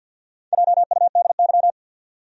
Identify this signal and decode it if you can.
Morse code, heard as a keyed tone, about 52 wpm, 700 Hz. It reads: JUDY
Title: USA Common Names - Max 5 Letters - 50wpm